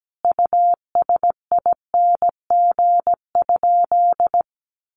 Morse code, heard as a keyed tone, about 17 wpm, 700 Hz. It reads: USING?
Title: Common QSO Phrases 17wpm